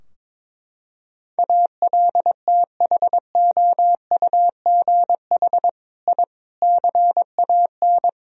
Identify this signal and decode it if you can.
Morse code, heard as a keyed tone, about 22 wpm, 700 Hz. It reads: ALTHOUGH I CAN
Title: Sentences from Top 1000 Words - Review Part 3 - 22wpm